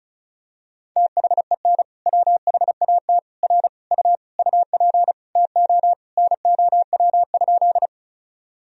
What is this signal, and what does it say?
Morse code, heard as a keyed tone, about 36 wpm, 700 Hz. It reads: THEN WHAT R U UP TO NOW?
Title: Sentences from Top 100 Words and Abbreviations 35wpm